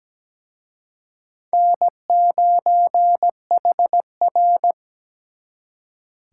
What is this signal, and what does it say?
Morse code, heard as a keyed tone, about 20 wpm, 700 Hz. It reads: N9HR
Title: Speed Racing - USA 1x2 Call Signs 17wpm